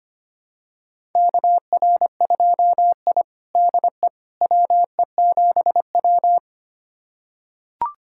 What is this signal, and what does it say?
Morse code, heard as a keyed tone, about 25 wpm, 700 Hz. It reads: KR2I DE WE7W
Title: USA Callsign DE Callsign - 4 Characters - 25wpm